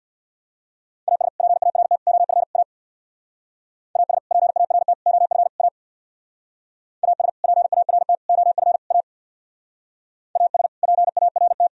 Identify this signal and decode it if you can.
Morse code, heard as a keyed tone, about 64 wpm, 700 Hz. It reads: AS PART OF A AS PART OF A AS PART OF A AS PART